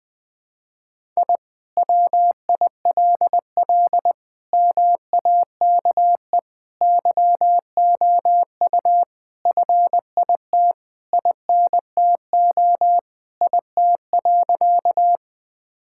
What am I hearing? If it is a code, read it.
Morse code, heard as a keyed tone, about 20 wpm, 700 Hz. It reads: I WILL MAKE YOU FIT INTO IT.
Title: Sentences from Top 1000 Words - Review Part 5 - 20wpm